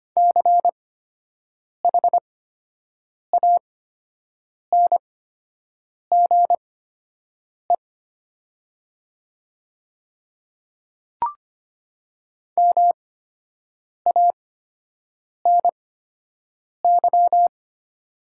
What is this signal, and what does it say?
Morse code, heard as a keyed tone, about 25 wpm, spaced as for 7 wpm, 700 Hz. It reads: CHANGE MANY
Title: 8x Character Spacing - Top 200 Words - 25wpm